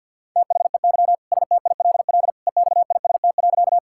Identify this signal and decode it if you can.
Morse code, heard as a keyed tone, about 53 wpm, 690 Hz. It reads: THEY STILL EXIST.
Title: Sentences from Top 1000 Words - Review Part 5 - 50wpm